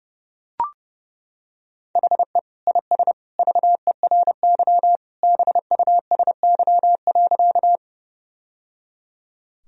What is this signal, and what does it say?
Morse code, heard as a keyed tone, about 28 wpm, 700 Hz. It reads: HE IS VERY BUSY.
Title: Sentences from Top 1000 Words - Review Part 3 - 30wpm